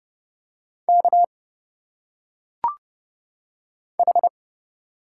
Morse code, heard as a keyed tone, about 30 wpm, 700 Hz. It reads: K H
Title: Single Letters - Phonetic Pronunciation 30wpm